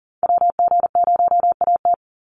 Morse code, heard as a keyed tone, about 40 wpm, 700 Hz. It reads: WG0AT
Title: NA SOTA Chaser Callsigns 40wpm